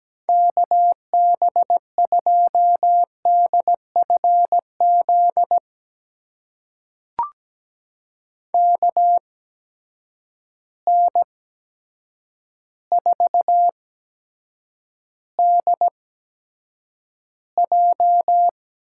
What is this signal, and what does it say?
Morse code, heard as a keyed tone, about 17 wpm, spaced as for 8 wpm, 700 Hz. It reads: KB2DFZ KN4DJ
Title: USA Call Signs - 6 Characters - 8x Character Spacing - 17wpm